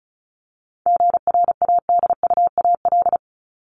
Morse code, heard as a keyed tone, about 34 wpm, 700 Hz. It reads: GRADUAL
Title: LY Suffix - Top 100 Words with Base Words 35wpm